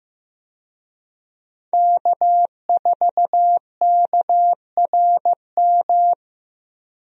Morse code, heard as a keyed tone, about 15 wpm, 700 Hz. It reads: K4KRM
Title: USA Call Signs - 5 Characters - Repeated 3x - 15wpm